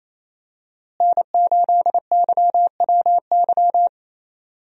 Morse code, heard as a keyed tone, about 27 wpm, 700 Hz. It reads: N8YWY
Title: USA Call Signs - 5 Characters - Repeated 3x - 28wpm